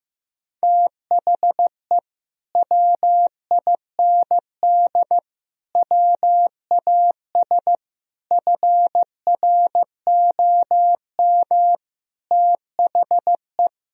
Morse code, heard as a keyed tone, about 16 wpm, 700 Hz. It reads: THE WIND WAS FROM THE
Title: Sentences from Top 1000 Words - Review Part 2 - 15wpm